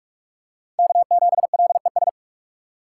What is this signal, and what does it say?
Morse code, heard as a keyed tone, about 46 wpm, 700 Hz. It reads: K7LES
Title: USA Call Signs - 5 Characters - 4x Character Spacing - 45wpm